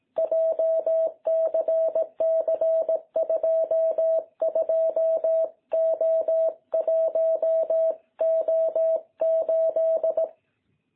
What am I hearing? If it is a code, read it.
Morse code, heard as a keyed tone, about 17 wpm, 660 Hz. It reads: JCC22O1O8
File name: jh8cbh / 2021cw